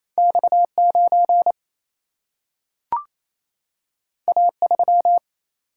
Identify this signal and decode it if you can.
Morse code, heard as a keyed tone, about 28 wpm, 700 Hz. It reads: X 9 A 3
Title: Single-Letter Single-Number 28wpm